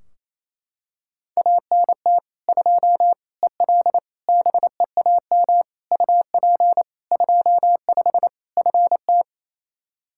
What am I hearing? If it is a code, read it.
Morse code, heard as a keyed tone, about 28 wpm, 700 Hz. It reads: ANT 2 EL BEAM UP 25 FT
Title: Practice QSO - Exchange 2 - Info - 28wpm